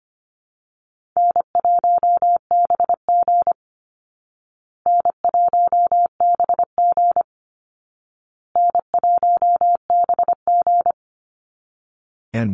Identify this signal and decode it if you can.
Morse code, heard as a keyed tone, about 25 wpm, 700 Hz. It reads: N1BG N1BG N1BG
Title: Easy Call Signs Repeated 3x - 25wpm